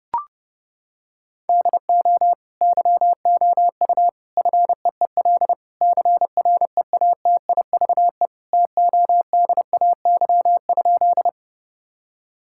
Morse code, heard as a keyed tone, about 30 wpm, 700 Hz. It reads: DO YOU FEEL CREATIVE TODAY?
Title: Sentences from Top 1000 Words - Review Part 5 - 30wpm